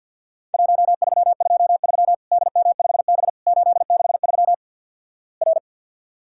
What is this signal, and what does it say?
Morse code, heard as a keyed tone, about 51 wpm, 700 Hz. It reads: 1323 DM5B 863 R
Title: Worked All Europe DX Contest - QTC Exchange - 50wpm